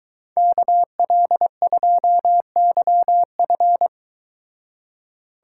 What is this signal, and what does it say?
Morse code, heard as a keyed tone, about 23 wpm, 700 Hz. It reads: KL2YF